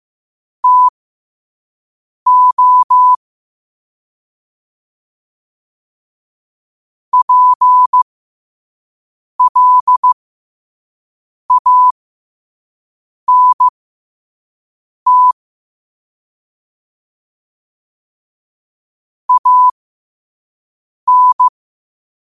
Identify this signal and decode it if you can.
Morse code, heard as a keyed tone, about 15 wpm, spaced as for 5 wpm, 1000 Hz. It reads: TO PLANT AN
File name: qotdfiles / 5wpm